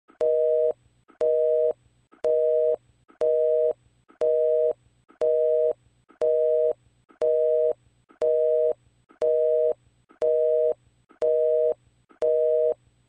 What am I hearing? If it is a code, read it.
Busy tone (North America), 480 + 620 Hz, with noise.
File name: Busy-Signal-Interrupted-Call-Sound-Effect